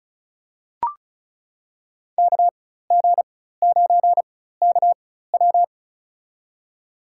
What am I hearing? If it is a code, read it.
Morse code, heard as a keyed tone, about 36 wpm, spaced as for 16 wpm, 700 Hz. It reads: KG9KW
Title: USA Call Signs - 5 Characters - 4x Character Spacing - 35wpm